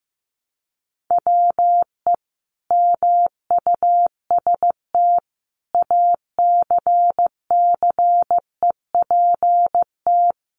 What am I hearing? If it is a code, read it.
Morse code, heard as a keyed tone, about 15 wpm, 700 Hz. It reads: WE MUST ACCEPT
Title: Sentences from Top 1300 Words 15wpm